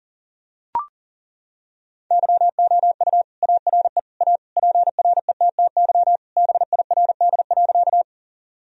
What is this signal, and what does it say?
Morse code, heard as a keyed tone, about 37 wpm, 700 Hz. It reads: YOU ARE A PRETTY BIRD.
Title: Sentences from Top 1000 Words - Review Part 2 - 40wpm